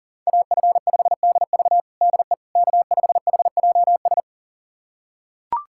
Morse code, heard as a keyed tone, about 37 wpm, 700 Hz. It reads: AF5DV DE K5HJS
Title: USA Callsign DE Callsign - 5 Characters - 40wpm